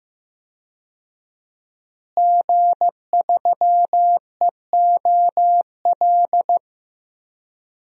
Morse code, heard as a keyed tone, about 15 wpm, 700 Hz. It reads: G3EOL
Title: DXCC-100 Call Signs - 5 Characters - Repeated 3x - 15wpm